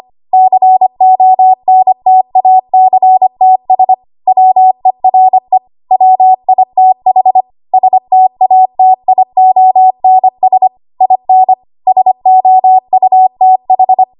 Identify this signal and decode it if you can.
Morse code, heard as a keyed tone, about 25 wpm, 750 Hz. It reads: CONTACTS WERE WITH STATIONS IN SOUTH
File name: Archive / 25 WPM / 25WPM